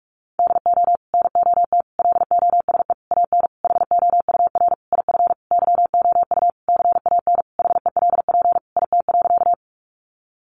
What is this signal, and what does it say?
Morse code, heard as a keyed tone, about 46 wpm, 700 Hz. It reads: DO NOT LOSE AN HOUR IF YOU CAN HELP IT.
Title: Sentences from Top 1100 Words 45wpm